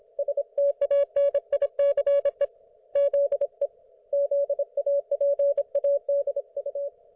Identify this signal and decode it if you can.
Morse code, heard as a keyed tone, about 27 wpm, 560 Hz. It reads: STANICE ZE ZAPADU